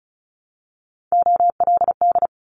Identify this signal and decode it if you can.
Morse code, heard as a keyed tone, about 34 wpm, 700 Hz. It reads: OLD
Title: Top 300 Words 35wpm